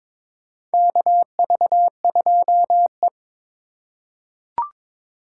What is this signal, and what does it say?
Morse code, heard as a keyed tone, about 22 wpm, 700 Hz. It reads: KV2E
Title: USA Call Signs - 4 Characters - 3x Character Spacing - 22wpm